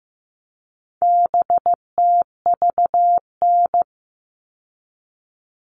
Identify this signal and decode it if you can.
Morse code, heard as a keyed tone, about 15 wpm, 700 Hz. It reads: BTVN
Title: All Cut Numbers - 4-Digit 15wpm